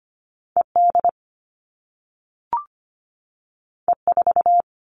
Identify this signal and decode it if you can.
Morse code, heard as a keyed tone, about 25 wpm, 700 Hz. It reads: ED E4
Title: All Cut Numbers - 2-Digit 25wpm